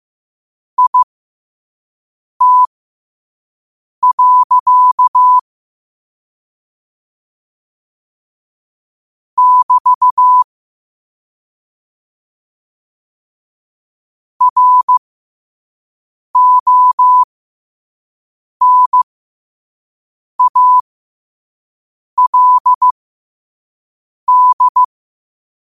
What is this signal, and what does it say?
Morse code, heard as a keyed tone, about 15 wpm, spaced as for 5 wpm, 1000 Hz. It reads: IT. <BT> RONALD
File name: qotdfiles / 5wpm